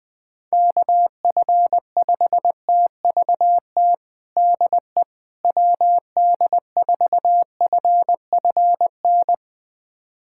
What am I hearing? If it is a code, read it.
Morse code, heard as a keyed tone, about 21 wpm, 700 Hz. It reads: KF5TVT DE WD4FFN
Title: USA Callsign DE Callsign - 6 Characters - 20wpm